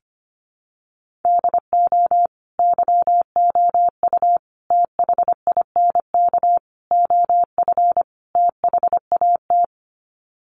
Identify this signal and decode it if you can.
Morse code, heard as a keyed tone, about 25 wpm, 700 Hz. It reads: DO YOU THINK OF THAT
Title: Sets of 5 Words Encore2 25wpm